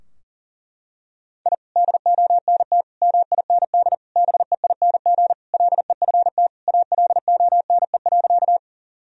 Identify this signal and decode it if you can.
Morse code, heard as a keyed tone, about 40 wpm, 700 Hz. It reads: I DONT MIND BEING LEFT ALONE.
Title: Sentences from Top 1000 Words - Review Part 3 - 40wpm